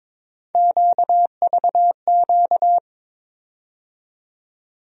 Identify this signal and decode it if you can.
Morse code, heard as a keyed tone, about 22 wpm, 700 Hz. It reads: QVQ
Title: USA Call Sign Postfixes - 3 Letters 22wpm